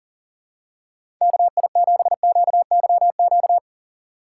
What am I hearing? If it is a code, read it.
Morse code, heard as a keyed tone, about 40 wpm, 700 Hz. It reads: KI7QYQ
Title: USA Call Signs - 3x Character Spacing - 40wpm